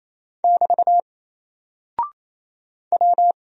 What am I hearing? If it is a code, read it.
Morse code, heard as a keyed tone, about 28 wpm, 700 Hz. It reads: <BT> W
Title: Single Character (Letter:Number:Punctuation:Prosign) Alt 28wpm